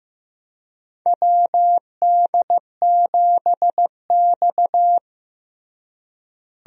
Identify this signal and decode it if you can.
Morse code, heard as a keyed tone, about 15 wpm, 700 Hz. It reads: WD7X
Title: USA Call Signs - 4 Characters - Repeated 2x - 15wpm